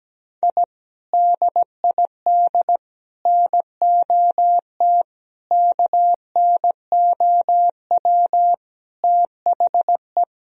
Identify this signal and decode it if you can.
Morse code, heard as a keyed tone, about 17 wpm, 700 Hz. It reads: I DID NOT KNOW THE
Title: Sentences from Top 1000 Words - Review Part 1 - 17wpm